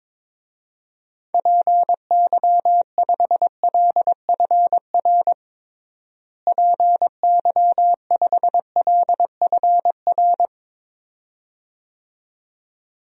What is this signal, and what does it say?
Morse code, heard as a keyed tone, about 22 wpm, 700 Hz. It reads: PY5LFR PY5LFR
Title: DXCC-100 Call Signs - 6 Characters - Repeated 2x - 22wpm